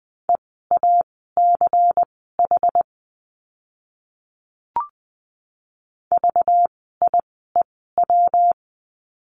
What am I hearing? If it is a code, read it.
Morse code, heard as a keyed tone, about 20 wpm, spaced as for 14 wpm, 700 Hz. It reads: EACH VIEW
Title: 192 Lesson 31f - Letter v - Words 20wpm